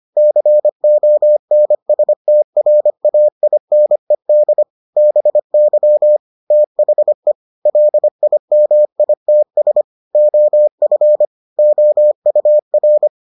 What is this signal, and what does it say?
Morse code, heard as a keyed tone, about 25 wpm, 600 Hz. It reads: CONSTRAINED BY THE LIMITS OF OUR